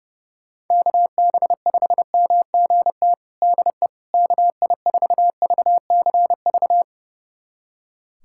Morse code, heard as a keyed tone, about 32 wpm, 700 Hz. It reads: KB5MGT DE KI4VCV
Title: USA Callsign DE Callsign - 6 Characters - 30wpm